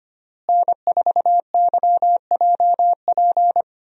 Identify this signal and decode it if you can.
Morse code, heard as a keyed tone, about 25 wpm, 700 Hz. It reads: N4YJP
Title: USA Call Signs - 5 Characters - Repeated 3x - 25wpm